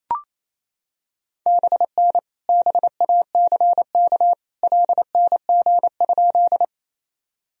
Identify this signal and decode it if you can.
Morse code, heard as a keyed tone, about 28 wpm, 700 Hz. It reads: BN BACK LNG?
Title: Sentences from Top 200 Words and Abbreviations 28wpm